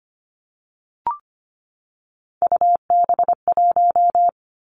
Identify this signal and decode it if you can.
Morse code, heard as a keyed tone, about 25 wpm, 700 Hz. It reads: UB1
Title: Three Challenging Letter-Number Combinations 25wpm